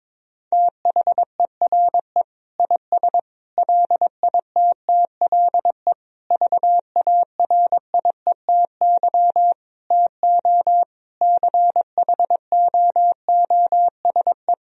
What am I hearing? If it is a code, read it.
Morse code, heard as a keyed tone, about 22 wpm, 700 Hz. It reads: THERE IS LITTLE VARIETY TO CHOOSE
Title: Sentences from Top 1000 Words - Review Part 4 - 22wpm